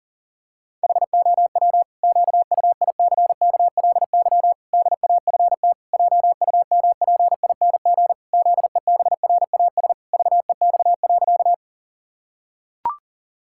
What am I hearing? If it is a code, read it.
Morse code, heard as a keyed tone, about 37 wpm, 700 Hz. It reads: HOW QUICKLY DAFT JUMPING ZEBRAS VEX.
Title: Pangram Challenge v2 - 40wpm